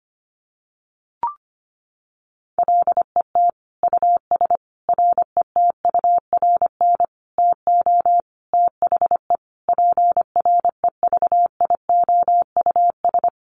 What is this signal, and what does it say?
Morse code, heard as a keyed tone, about 25 wpm, 700 Hz. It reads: LET US RETURN TO THE PREVIOUS
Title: Sentences from Top 1500 Words 25wpm